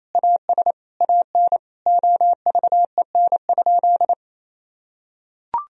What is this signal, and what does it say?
Morse code, heard as a keyed tone, about 28 wpm, 700 Hz. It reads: AS AN OVEN?
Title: Sentences from Top 1000 Words - Review Part 4 - 28wpm